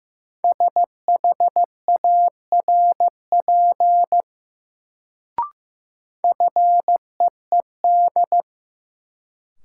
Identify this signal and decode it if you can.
Morse code, heard as a keyed tone, about 15 wpm, 700 Hz. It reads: SHARP FEED
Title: Sentences from Top 1600 Words - New Word Drill Alt - 15wpm